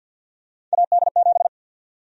Morse code, heard as a keyed tone, about 52 wpm, 700 Hz. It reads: AD7
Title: USA Call Sign Prefixes - Repeated 2x - 50wpm